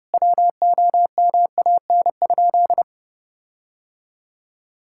Morse code, heard as a keyed tone, about 32 wpm, 700 Hz. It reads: WOMAN?
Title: Sentences from Top 1600 Words 30wpm